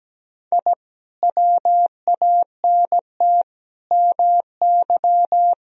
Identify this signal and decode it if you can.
Morse code, heard as a keyed tone, about 17 wpm, 700 Hz. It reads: I WANT MY
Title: Sentences from Top 1000 Words - Review Part 2 - 17wpm